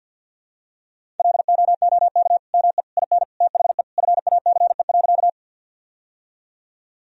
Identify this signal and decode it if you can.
Morse code, heard as a keyed tone, about 52 wpm, 700 Hz. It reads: LOOK ME IN THE FACE.
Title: Sentences from Top 1000 Words - Review Part 2 - 50wpm